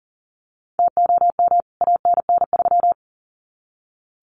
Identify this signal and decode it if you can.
Morse code, heard as a keyed tone, about 40 wpm, 700 Hz. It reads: TOM ANN3
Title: CWT Exchange 40wpm